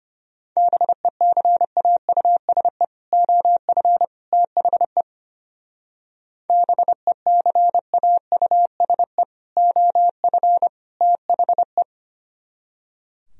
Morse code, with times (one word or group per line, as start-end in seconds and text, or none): 0.6-2.8 BECAUSE
3.1-4.0 OF
4.3-5.0 THE
6.5-9.2 BECAUSE
9.6-10.7 OF
11.0-11.8 THE